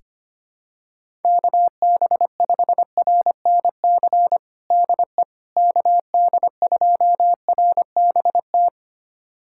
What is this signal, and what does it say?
Morse code, heard as a keyed tone, about 25 wpm, 700 Hz. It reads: KB5RNC DE KD2RBT
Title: USA Callsign DE Callsign - 6 Characters - 25wpm